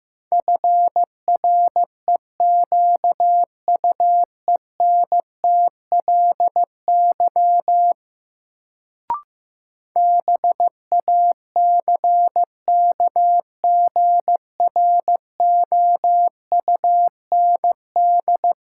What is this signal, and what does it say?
Morse code, heard as a keyed tone, about 15 wpm, 700 Hz. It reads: FREQUENTLY BACKGROUND
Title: Sentences from Top 1300 Words - New Word Drill Alt - 15wpm